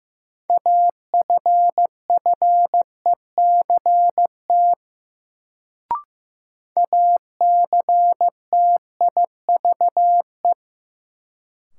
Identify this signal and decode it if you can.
Morse code, heard as a keyed tone, about 15 wpm, 700 Hz. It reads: AFFECT ACTIVE
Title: Sentences from Top 900 Words - New Word Drill - 15wpm